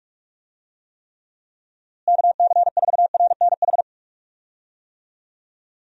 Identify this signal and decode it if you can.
Morse code, heard as a keyed tone, about 41 wpm, 700 Hz. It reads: KC4RNH
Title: USA Call Signs - 6 Characters - Repeated 3x - 45wpm